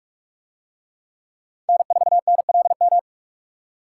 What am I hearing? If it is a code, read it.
Morse code, heard as a keyed tone, about 41 wpm, 700 Hz. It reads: N4NLM